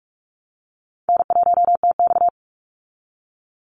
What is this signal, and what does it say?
Morse code, heard as a keyed tone, about 44 wpm, 700 Hz. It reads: N1TX
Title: Call Signs Repeated 3x - 45wpm